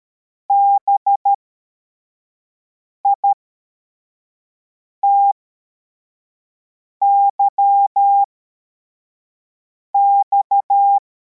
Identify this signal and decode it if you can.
Morse code, heard as a keyed tone, about 13 wpm, spaced as for 4 wpm, 800 Hz. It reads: BITYX
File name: formation / cw / Text74